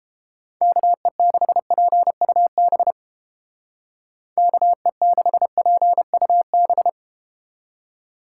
Morse code, heard as a keyed tone, about 31 wpm, 700 Hz. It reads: KE6PUB KE6PUB